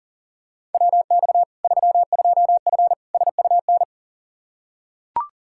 Morse code, heard as a keyed tone, about 40 wpm, 700 Hz. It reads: WX 32F SUN
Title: Practice QSO - Exchange 2 - Info - 40wpm